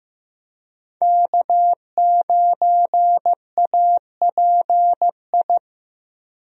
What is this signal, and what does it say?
Morse code, heard as a keyed tone, about 15 wpm, 700 Hz. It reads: K9API
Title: USA Call Signs - 5 Characters - Repeated 3x - 15wpm